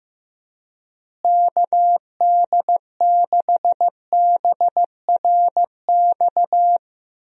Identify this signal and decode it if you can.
Morse code, heard as a keyed tone, about 15 wpm, 700 Hz. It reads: KD6BRX
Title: USA Call Signs - 6 Characters - Repeated 3x - 15wpm